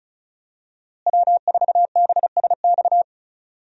Morse code, heard as a keyed tone, about 34 wpm, 700 Hz. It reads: W4BSX
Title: USA Call Signs - 5 Characters - Repeated 2x - 35wpm